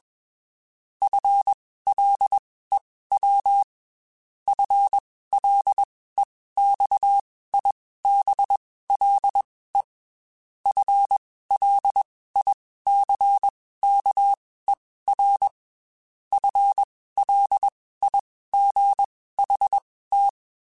Morse code, heard as a keyed tone, about 21 wpm, spaced as for 15 wpm, 800 Hz. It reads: FLEW FLEXIBLE FLICKER FLIGHT
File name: Spaced / FL Runs sp mp3s / FL Runs sp @20